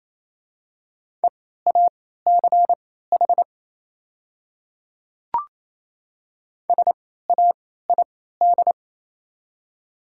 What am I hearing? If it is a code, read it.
Morse code, heard as a keyed tone, about 28 wpm, spaced as for 16 wpm, 700 Hz. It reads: EACH SAID
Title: 3x Character Spacing - Top 100 Words - 28wpm